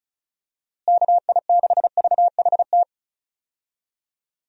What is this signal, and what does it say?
Morse code, heard as a keyed tone, about 34 wpm, 700 Hz. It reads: KI6VHT
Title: USA Call Signs - 6 Characters - Repeated 3x - 35wpm